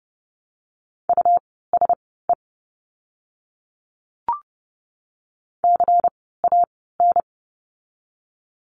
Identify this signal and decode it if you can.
Morse code, heard as a keyed tone, about 30 wpm, spaced as for 17 wpm, 700 Hz. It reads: USE CAN
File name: Lesson 17f - Letter u - Words 30wpm